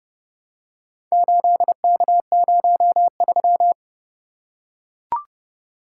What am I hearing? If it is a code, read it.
Morse code, heard as a keyed tone, about 29 wpm, 700 Hz. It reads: 8K03